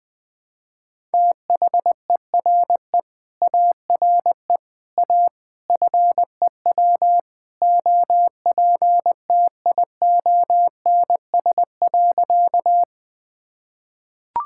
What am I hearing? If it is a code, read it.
Morse code, heard as a keyed tone, about 21 wpm, 700 Hz. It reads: THERE ARE A FEW OPTIONS.
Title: Sentences with 3-Word Phrases - 20wpm